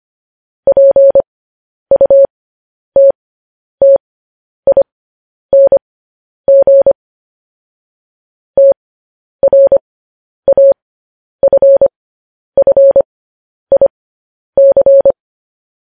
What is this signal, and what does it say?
Morse code, heard as a keyed tone, about 25 wpm, spaced as for 10 wpm, 550 Hz. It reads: PUTTING TRAFFIC
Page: news-headlines-in-morse-code-at-10-wpm